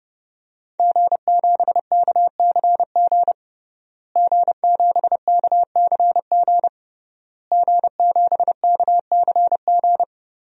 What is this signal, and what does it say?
Morse code, heard as a keyed tone, about 30 wpm, 700 Hz. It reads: G7KCG G7KCG G7KCG
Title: DXCC-100 Call Signs - 5 Characters - Repeated 3x - 30wpm